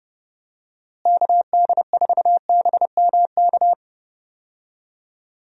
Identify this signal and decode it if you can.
Morse code, heard as a keyed tone, about 31 wpm, 700 Hz. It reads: KD4BMK
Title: USA Call Signs - 6 Characters - Repeated 3x - 30wpm